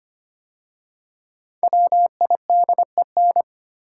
Morse code, heard as a keyed tone, about 25 wpm, 700 Hz. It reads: WIDEN
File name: 5 Letter Words - First 3 Letters Given - 25wpm